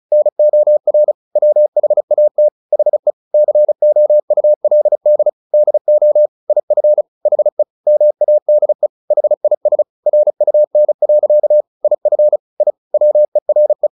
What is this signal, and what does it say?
Morse code, heard as a keyed tone, about 34 wpm, 600 Hz. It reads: NOR WHAT HE COULD DO IF HE MADE HIS RUN. IF I WERE
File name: the old man and the sea / the old man and the sea 1 35wpm